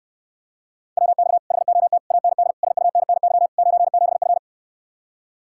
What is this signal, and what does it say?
Morse code, heard as a keyed tone, about 69 wpm, 700 Hz. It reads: WX HOT AND SUNNY 97F